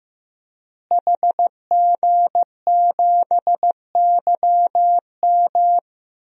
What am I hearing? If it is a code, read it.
Morse code, heard as a keyed tone, about 15 wpm, 700 Hz. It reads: HG7YM
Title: DXCC-100 Call Signs - 5 Characters - Repeated 2x - 15wpm